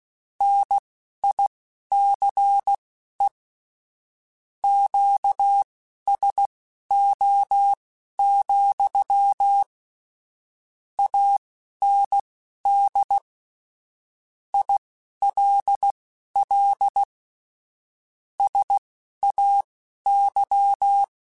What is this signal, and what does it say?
Morse code, heard as a keyed tone, about 16 wpm, spaced as for 11 wpm, 800 Hz. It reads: NICE QSO, AND ILL SAY
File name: QSOs / QSOs-mp3s / WA7QOX 2@10